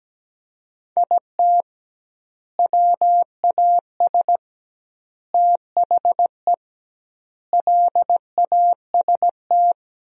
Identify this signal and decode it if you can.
Morse code, heard as a keyed tone, about 17 wpm, 700 Hz. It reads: IT WAS THE LAST
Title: 2x Word-Spacing - Sentences from Top 300 Words - 17wpm